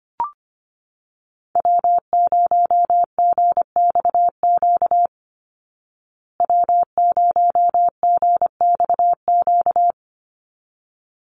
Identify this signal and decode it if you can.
Morse code, heard as a keyed tone, about 25 wpm, 700 Hz. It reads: W0GXQ W0GXQ
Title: Call Signs Repeated 3x - 25wpm